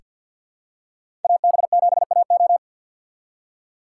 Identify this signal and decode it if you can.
Morse code, heard as a keyed tone, about 48 wpm, 700 Hz. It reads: AB7AO
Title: USA Call Signs - 3x Character Spacing - 50wpm